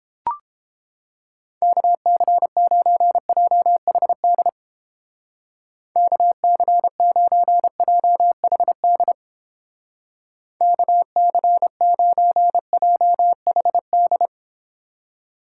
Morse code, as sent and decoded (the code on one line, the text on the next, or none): -.- -.-. ----. .--- .... -.. / -.- -.-. ----. .--- .... -.. / -.- -.-. ----. .--- .... -..
KC9JHD KC9JHD KC9JHD